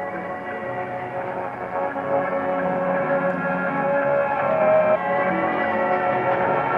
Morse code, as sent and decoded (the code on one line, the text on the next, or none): none